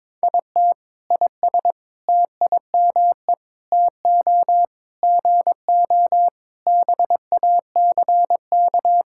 Morse code, heard as a keyed tone, about 22 wpm, 700 Hz. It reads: IT IS TIME TO GO BACK
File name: Sentences with 7-Word Phrases - 22wpm